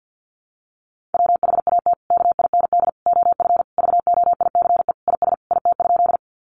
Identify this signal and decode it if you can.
Morse code, heard as a keyed tone, about 50 wpm, 700 Hz. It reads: WHAT KIND OF VOICE IS IT?
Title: Sentences from Top 500 Words 50wpm